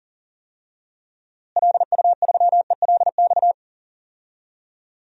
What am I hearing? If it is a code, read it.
Morse code, heard as a keyed tone, about 40 wpm, 700 Hz. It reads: LU3ELX
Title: DXCC-100 Call Signs - 6 Characters - Repeated 3x - 40wpm